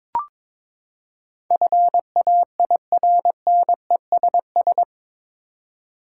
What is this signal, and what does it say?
Morse code, heard as a keyed tone, about 22 wpm, 700 Hz. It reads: FAIRNESS
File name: practice / 22wpm / Eight Letter Words - 22wpm